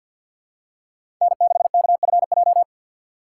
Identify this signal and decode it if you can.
Morse code, heard as a keyed tone, about 53 wpm, 700 Hz. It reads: N6KFJ
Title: USA Call Signs - 5 Characters - 4x Character Spacing - 50wpm